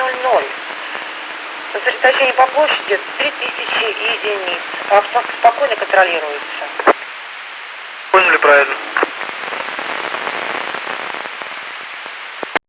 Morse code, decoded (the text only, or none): none